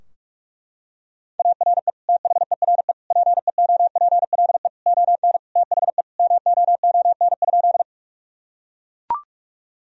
Morse code, heard as a keyed tone, about 41 wpm, 700 Hz. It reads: ARE THERE PEOPLE ON THE MOON?
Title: Sentences from Top 1000 Words - Review Part 3 - 45wpm